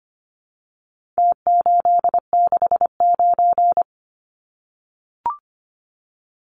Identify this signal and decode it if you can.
Morse code, heard as a keyed tone, about 25 wpm, 700 Hz. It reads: T869
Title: Numbers - Contest Style - 4-Digit 25wpm